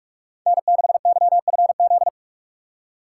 Morse code, heard as a keyed tone, about 46 wpm, 700 Hz. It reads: N6YFZ
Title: USA Call Signs - 5 Characters - 3x Character Spacing - 45wpm